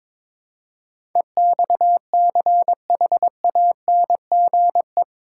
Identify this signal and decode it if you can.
Morse code, heard as a keyed tone, about 22 wpm, 700 Hz. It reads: EXCHANGE
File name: practice / 22wpm / Eight Letter Words - 22wpm